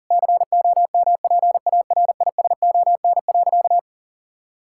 Morse code, heard as a keyed tone, about 40 wpm, 700 Hz. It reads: COMPARISON.
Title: Sentences from Top 1600 Words 40wpm